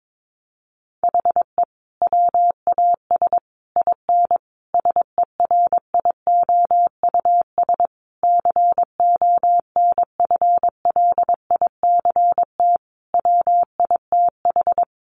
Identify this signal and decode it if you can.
Morse code, heard as a keyed tone, about 22 wpm, 700 Hz. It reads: HE WAS IN SERIOUS CONFLICT WITH
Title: Sentences from Top 1300 Words 22wpm